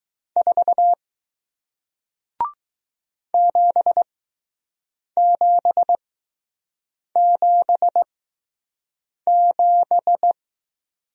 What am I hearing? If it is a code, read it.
Morse code, heard as a keyed tone, about 20 wpm, spaced as for 6 wpm, 700 Hz. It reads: 4 7777